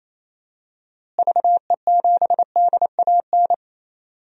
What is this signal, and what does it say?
Morse code, heard as a keyed tone, about 28 wpm, 700 Hz. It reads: VE7DAN
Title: DXCC-100 Call Signs - 6 Characters - Repeated 2x - 28wpm